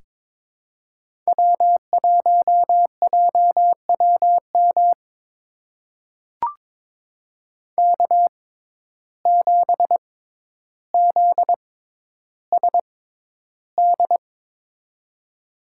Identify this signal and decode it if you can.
Morse code, heard as a keyed tone, about 22 wpm, spaced as for 11 wpm, 700 Hz. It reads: W1JWM K7ZSD
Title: USA Call Signs - 6x Character Spacing - 22wpm